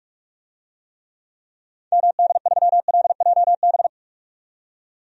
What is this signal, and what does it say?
Morse code, heard as a keyed tone, about 42 wpm, 700 Hz. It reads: MD3LJB
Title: DXCC-100 Call Signs - 6 Characters - 45wpm